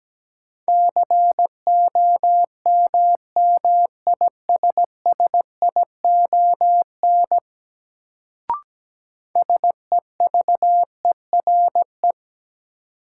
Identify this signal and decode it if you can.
Morse code, heard as a keyed tone, about 17 wpm, 700 Hz. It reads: COMMISSION SEVERE
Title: Sentences from Top 1400 Words - New Word Drill Alt - 17wpm